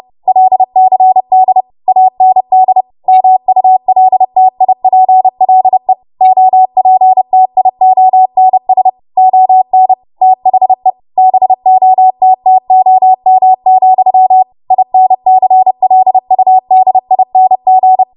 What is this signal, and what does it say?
Morse code, heard as a keyed tone, about 30 wpm, 750 Hz. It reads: LCD AND MULTIPLE OPTIONS ON THE BOTTOM, INCLUDING
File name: Morse / 30WPM